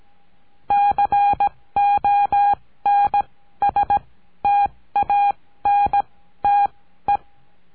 Morse code, heard as a keyed tone, about 17 wpm, spaced as for 13 wpm, 810 Hz. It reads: CONSTANTE
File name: CURSO DE TELEGRAFIA